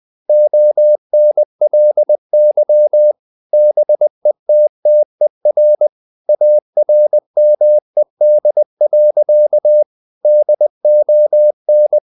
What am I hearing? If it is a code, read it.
Morse code, heard as a keyed tone, about 20 wpm, 600 Hz. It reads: ONLY BETTER ARMED. DON
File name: the old man and the sea part 2 20wpm